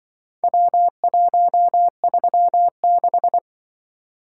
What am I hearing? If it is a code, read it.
Morse code, heard as a keyed tone, about 24 wpm, 700 Hz. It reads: W136
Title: Speed-Racing - Four Letter-Number Combinations - 20wpm